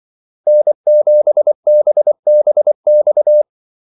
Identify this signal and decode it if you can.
Morse code, heard as a keyed tone, about 24 wpm, 600 Hz. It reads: N7BBX
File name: Callsigns-24wpm